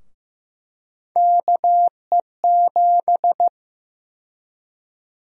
Morse code, heard as a keyed tone, about 15 wpm, 700 Hz. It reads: KE7
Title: USA Call Sign Prefixes - 15wpm